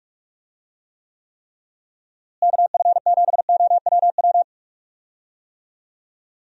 Morse code, heard as a keyed tone, about 45 wpm, 700 Hz. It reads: KF7OWW
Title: USA Call Signs - 6 Characters - Repeated 3x - 45wpm